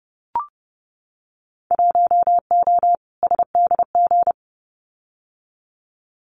Morse code, heard as a keyed tone, about 30 wpm, 700 Hz. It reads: 1O SDG
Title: Winter Field Day Exchanges Encore 30wpm